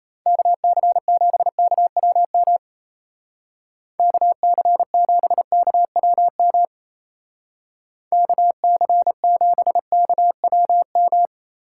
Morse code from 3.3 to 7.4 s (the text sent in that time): KC7KWM